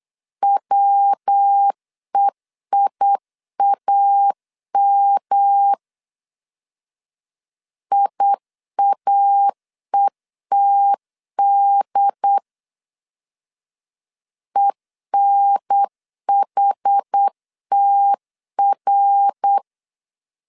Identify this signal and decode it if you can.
Morse code, heard as a keyed tone, about 8 wpm, 800 Hz. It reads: WEIAM IAETD ENHTR
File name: CW / DET / 1 3 1500C S B 8wpm 4141